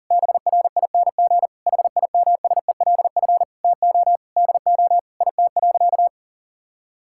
Morse code, heard as a keyed tone, about 40 wpm, 700 Hz. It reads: BRING HIMSELF TO DO IT.
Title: Sentences from Top 1600 Words 40wpm